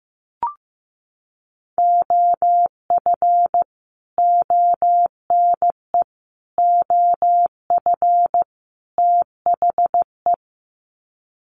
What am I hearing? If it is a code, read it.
Morse code, heard as a keyed tone, about 14 wpm, 700 Hz. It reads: OF ONE OF THE
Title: Sets of 4 Words 15wpm